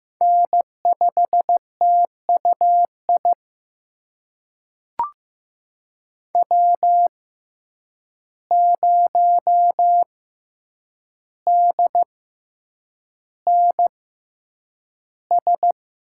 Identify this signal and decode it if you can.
Morse code, heard as a keyed tone, about 15 wpm, spaced as for 8 wpm, 700 Hz. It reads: N5TUI W0DNS
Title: USA Call Signs - 5 Characters - 6x Character Spacing - 15wpm